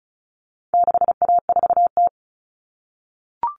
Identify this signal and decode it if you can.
Morse code, heard as a keyed tone, about 34 wpm, 700 Hz. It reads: 6A4T
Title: All Cut Numbers - 4-Digit 35wpm